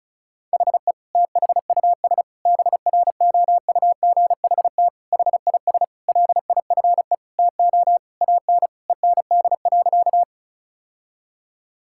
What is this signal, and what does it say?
Morse code, heard as a keyed tone, about 36 wpm, 700 Hz. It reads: HE THUS BROUGHT HIS LIFE TO AN END.
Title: Sentences from Top 1000 Words - Review Part 5 - 35wpm